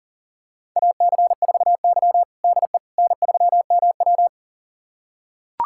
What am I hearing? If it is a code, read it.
Morse code, heard as a keyed tone, about 40 wpm, 700 Hz. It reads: AC4Y DE N3MW
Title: USA Callsign DE Callsign - 4 Characters - 40wpm